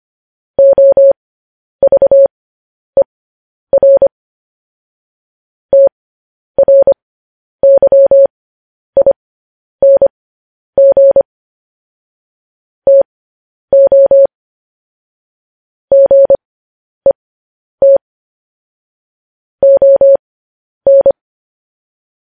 Morse code, heard as a keyed tone, about 25 wpm, spaced as for 10 wpm, 550 Hz. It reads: OVER TRYING TO GET ON
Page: news headlines in morse code at 10 wpm